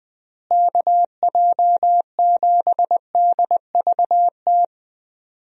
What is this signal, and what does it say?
Morse code, heard as a keyed tone, about 21 wpm, 700 Hz. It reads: KJ7DVT
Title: USA Call Signs - 6 Characters - Repeated 3x - 20wpm